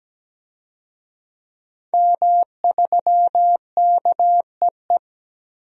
Morse code, heard as a keyed tone, about 17 wpm, 700 Hz. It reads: M3KEE
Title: DXCC-100 Call Signs - 5 Characters - 17wpm